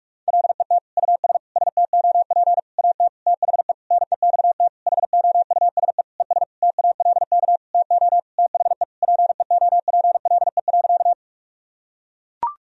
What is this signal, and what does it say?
Morse code, heard as a keyed tone, about 41 wpm, 700 Hz. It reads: LET US STOP AT THE NEXT HOUSE ES TALK TO THE PEOPLE.
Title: Sentences from Top 400 Words and Abbreviations 45wpm